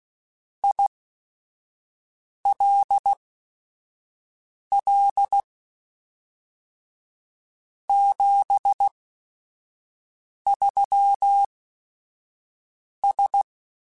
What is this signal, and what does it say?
Morse code, heard as a keyed tone, about 16 wpm, spaced as for 5 wpm, 800 Hz. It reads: ILL 73S